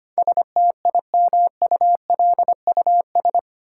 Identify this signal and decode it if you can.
Morse code, heard as a keyed tone, about 25 wpm, 700 Hz. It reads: STIMULUS
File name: Eight Letter Words - 25wpm